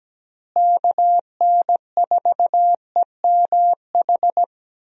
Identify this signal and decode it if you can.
Morse code, heard as a keyed tone, about 17 wpm, 700 Hz. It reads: KN4EMH
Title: USA Call Signs - 6 Characters - Repeated 3x - 17wpm